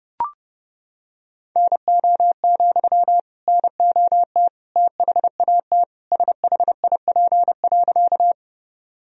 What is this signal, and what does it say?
Morse code, heard as a keyed tone, about 30 wpm, 700 Hz. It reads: NO, NOT THAT SHIP.
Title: Sentences from Top 1000 Words - Review Part 2 - 30wpm